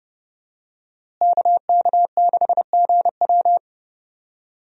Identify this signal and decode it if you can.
Morse code, heard as a keyed tone, about 30 wpm, 700 Hz. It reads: KK6GW
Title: USA Call Signs - 5 Characters - Repeated 3x - 30wpm